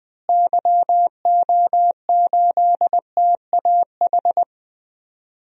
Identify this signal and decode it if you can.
Morse code, heard as a keyed tone, about 20 wpm, 700 Hz. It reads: YO8TAH
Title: DXCC-100 Call Signs - 6 Characters - Repeated 2x - 20wpm